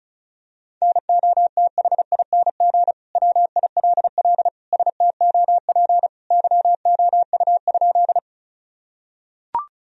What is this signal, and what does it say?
Morse code, heard as a keyed tone, about 34 wpm, 700 Hz. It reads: NOTHING WILL STOP YOU?
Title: Sentences from Top 1000 Words - Review Part 1 - 35wpm